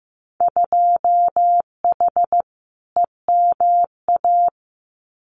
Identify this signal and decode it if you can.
Morse code, heard as a keyed tone, about 15 wpm, 700 Hz. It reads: 2H EMA
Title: Winter Field Day Exchanges 15wpm